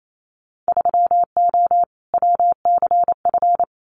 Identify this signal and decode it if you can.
Morse code, heard as a keyed tone, about 28 wpm, 700 Hz. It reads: 3O WCF
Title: Winter Field Day Exchanges 28wpm